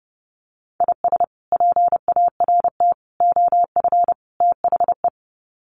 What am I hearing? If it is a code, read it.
Morse code, heard as a keyed tone, about 28 wpm, 700 Hz. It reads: IS PART OF THE
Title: Sets of 4 Words 30wpm